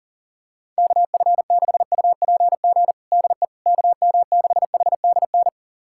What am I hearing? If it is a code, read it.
Morse code, heard as a keyed tone, about 40 wpm, 700 Hz. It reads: KF6UPG DE KM6HDN
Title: USA Callsign DE Callsign - 6 Characters - 40wpm